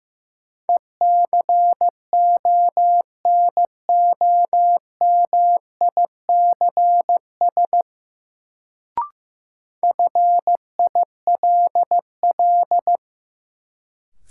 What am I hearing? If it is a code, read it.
Morse code, heard as a keyed tone, about 15 wpm, 700 Hz. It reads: ECONOMICS FILL
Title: Sentences from Top 1000 Words - Review Part 3 - New Word Drill - 15wpm